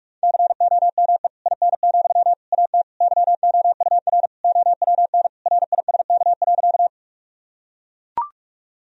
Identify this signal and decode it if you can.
Morse code, heard as a keyed tone, about 41 wpm, 700 Hz. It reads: COME IN, AT YOUR OWN RISK.
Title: Sentences from Top 1000 Words - Review Part 4 - 45wpm